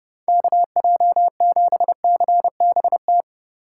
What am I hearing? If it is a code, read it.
Morse code, heard as a keyed tone, about 30 wpm, 700 Hz. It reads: KJ7CBT
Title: USA Call Signs - 4x Character Spacing - 30wpm